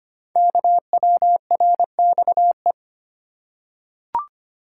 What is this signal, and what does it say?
Morse code, heard as a keyed tone, about 25 wpm, 700 Hz. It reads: KWRXE
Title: Code Groups - 5 Letters - 25wpm